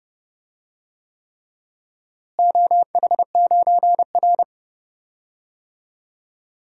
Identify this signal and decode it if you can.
Morse code, heard as a keyed tone, about 29 wpm, 700 Hz. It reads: OH9R
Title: DXCC-100 Call Signs - 4 Characters - Repeated 3x - 30wpm